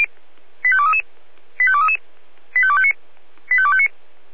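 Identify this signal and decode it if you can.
Five-tone selective call (ZVEI), heard as quick runs of five tones, tones of about 70 ms. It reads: 85233 85233 85259 85259
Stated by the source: ZVEI five-tone selective call received via SDR.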